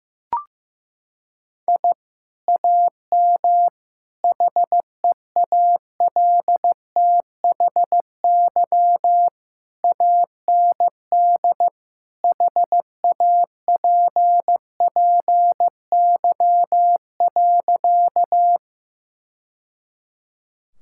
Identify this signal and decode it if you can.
Morse code, heard as a keyed tone, about 16 wpm, 700 Hz. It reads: I AM HEALTHY AND HAPPY.
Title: Sentences from Top 1000 Words - Review Part 5 - 15wpm